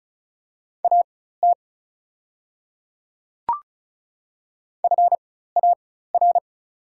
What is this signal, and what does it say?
Morse code, heard as a keyed tone, about 36 wpm, spaced as for 16 wpm, 700 Hz. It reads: AT FAR
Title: 4x Character Spacing - Top 500 Words - 35wpm